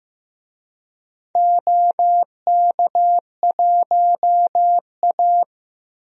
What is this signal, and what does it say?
Morse code, heard as a keyed tone, about 15 wpm, 700 Hz. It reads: OK1A
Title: DXCC-100 Call Signs - 4 Characters - Repeated 2x - 15wpm